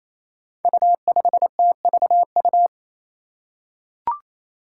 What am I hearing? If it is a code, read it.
Morse code, heard as a keyed tone, about 28 wpm, 700 Hz. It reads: U5TVU
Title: Five Challenging Letter-Number Combinations 28wpm